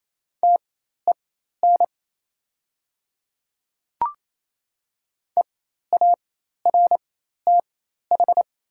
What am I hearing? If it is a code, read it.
Morse code, heard as a keyed tone, about 28 wpm, spaced as for 13 wpm, 700 Hz. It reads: TEN EARTH
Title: 4x Character Spacing - Top 500 Words - 28wpm